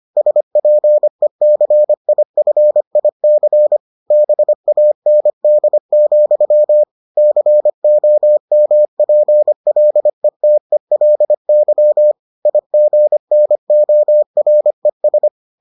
Morse code, heard as a keyed tone, about 25 wpm, 600 Hz. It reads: SPECIFIC BAND, COMPLETELY IGNORES